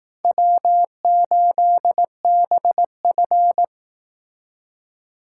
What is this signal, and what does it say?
Morse code, heard as a keyed tone, about 18 wpm, 700 Hz. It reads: W8BF
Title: Speed Racing - USA 1x2 Call Signs 15wpm